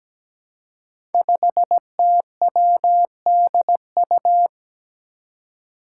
Morse code, heard as a keyed tone, about 17 wpm, 700 Hz. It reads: 5TWDU
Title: Five Challenging Letter-Number Combinations 17wpm